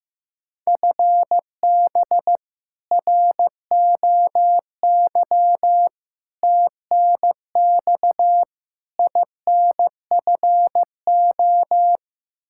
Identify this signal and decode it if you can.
Morse code, heard as a keyed tone, about 15 wpm, 700 Hz. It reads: FB ROY TNX INFO
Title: K1USN SST QSO Practice 15wpm